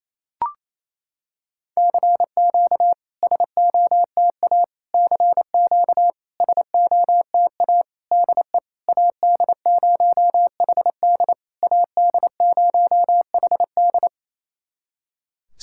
Morse code, heard as a keyed tone, about 28 wpm, 700 Hz. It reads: CQ SOTA CQ SOTA DE AD0HD AD0HD
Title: SOTA QSO Practice 28wpm